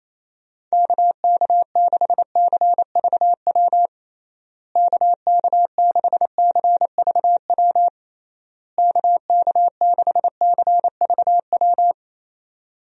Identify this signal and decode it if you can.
Morse code, heard as a keyed tone, about 28 wpm, 700 Hz. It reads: KK6CVW KK6CVW KK6CVW
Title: USA Call Signs - 6 Characters - Repeated 3x - 28wpm